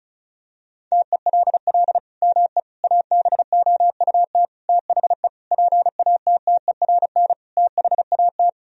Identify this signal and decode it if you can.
Morse code, heard as a keyed tone, about 36 wpm, 700 Hz. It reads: TELL ME ABOUT THE PATTERN THAT
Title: Sentences from Top 1000 Words - Review Part 2 - 35wpm